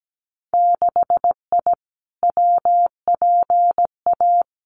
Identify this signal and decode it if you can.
Morse code, heard as a keyed tone, about 17 wpm, 700 Hz. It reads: 6I WPA
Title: Winter Field Day Exchanges Encore 17wpm